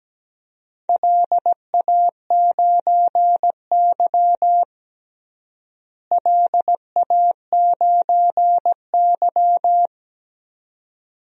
Morse code, heard as a keyed tone, about 17 wpm, 700 Hz. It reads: LA9Y LA9Y
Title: DXCC-100 Call Signs - 4 Characters - Repeated 2x - 17wpm